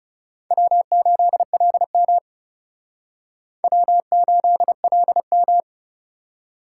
Morse code, heard as a keyed tone, about 37 wpm, 700 Hz. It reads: W8LM W8LM
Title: Speed Racing - USA 1x2 Call Signs 30wpm